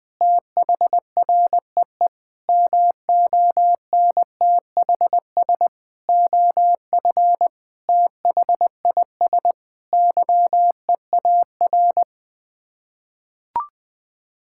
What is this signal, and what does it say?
Morse code, heard as a keyed tone, about 20 wpm, 700 Hz. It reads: THREE MONTHS OF THIS YEAR
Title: Sets of 8 Words - 20wpm